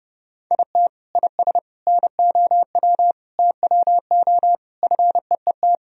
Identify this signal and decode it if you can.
Morse code, heard as a keyed tone, about 31 wpm, 700 Hz. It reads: IT IS NOW TWO FEET
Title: Sentences from Top 1000 Words - Review Part 1 - 30wpm